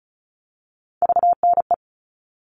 Morse code, heard as a keyed tone, about 34 wpm, 700 Hz. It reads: VNE
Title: All Cut Numbers - 3-Digit 35wpm